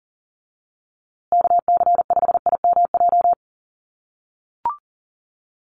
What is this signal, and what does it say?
Morse code, heard as a keyed tone, about 40 wpm, 700 Hz. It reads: KC5IMJ
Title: USA Call Signs Repeated 3x 40wpm